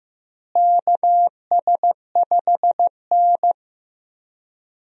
Morse code, heard as a keyed tone, about 15 wpm, 700 Hz. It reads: KS5N
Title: USA Call Signs - 4 Characters - Repeated 3x - 15wpm